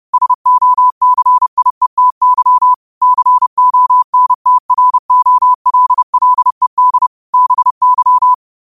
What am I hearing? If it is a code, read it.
Morse code, heard as a keyed tone, about 30 wpm, 1000 Hz. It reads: SOCIETY CONTROLLED BY